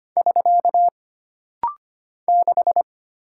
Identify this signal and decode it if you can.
Morse code, heard as a keyed tone, about 25 wpm, 700 Hz. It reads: <SK> 6
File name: Single Character (Letter:Number:Punctuation:Prosign) - Rapid-Fire - 25wpm